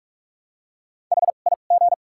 Morse code, heard as a keyed tone, about 44 wpm, 700 Hz. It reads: H I G
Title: 2x Character Spacing - Top 300 Words - 45wpm